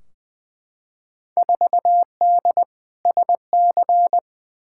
Morse code, heard as a keyed tone, about 21 wpm, 700 Hz. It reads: 4D SC
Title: Field Day Exchanges Encore - 20wpm